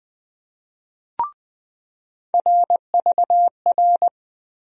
Morse code, heard as a keyed tone, about 20 wpm, 700 Hz. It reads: RVR
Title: USA Call Sign Postfixes - 3 Letters 20wpm